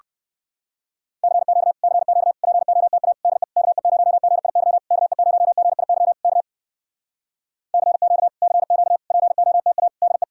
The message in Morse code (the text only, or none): CQ CQ POTA DE KE0GIY KE0GIY K CQ CQ POTA DE